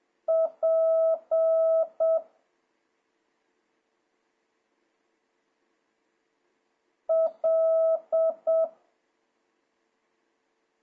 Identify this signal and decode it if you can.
Morse code, heard as a keyed tone, about 7 wpm, 650 Hz. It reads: P L
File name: jh8cbh / 2026cw